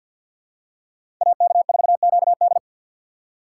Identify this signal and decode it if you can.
Morse code, heard as a keyed tone, about 47 wpm, 700 Hz. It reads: AK4QD
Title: USA Call Signs - 5 Characters - 4x Character Spacing - 50wpm